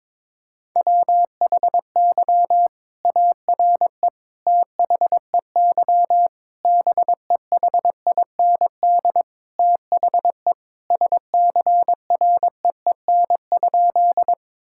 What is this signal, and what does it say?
Morse code, heard as a keyed tone, about 22 wpm, 700 Hz. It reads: WHY ARE THEY BEHIND THE SCREEN?
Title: Sentences from Top 1000 Words - Review Part 6 - 22wpm